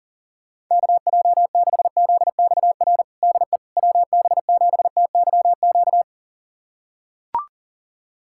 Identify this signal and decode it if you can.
Morse code, heard as a keyed tone, about 40 wpm, 700 Hz. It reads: KJ6ZXR DE WB7TYQ
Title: USA Callsign DE Callsign - 6 Characters - 40wpm